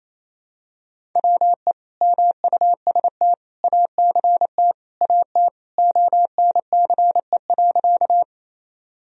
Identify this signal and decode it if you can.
Morse code, heard as a keyed tone, about 28 wpm, 700 Hz. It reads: WE MUST ACT AT ONCE.
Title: Sentences from Top 1000 Words - Review Part 1 - 28wpm